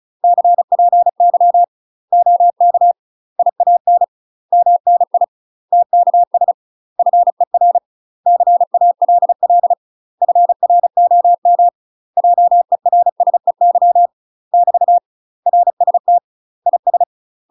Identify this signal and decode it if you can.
Morse code, heard as a keyed tone, about 34 wpm, 700 Hz. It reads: CPY OK IAN MNI TKS FER CALL FROM JERSEY <BT> RST IS
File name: UK MASTER QSOs TRAINING / 35WPM / UK MASTER QSOs 0025